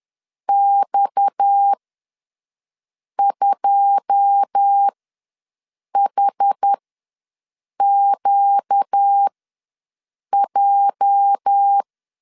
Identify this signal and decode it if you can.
Morse code, heard as a keyed tone, about 10 wpm, spaced as for 6 wpm, 800 Hz. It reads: X2HQJ
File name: LEZIONI / 9 2 ws 10wpm 4130